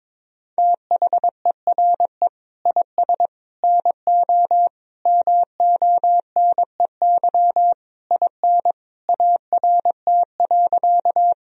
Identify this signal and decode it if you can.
Morse code, heard as a keyed tone, about 22 wpm, 700 Hz. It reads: THERE IS NO MONEY IN ART.
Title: Sentences from Top 1000 Words - Review Part 3 - 22wpm